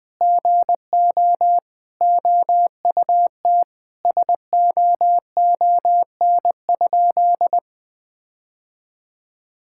Morse code, heard as a keyed tone, about 20 wpm, 700 Hz. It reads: GO OUT SOON?
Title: Sentences from Top 1600 Words 20wpm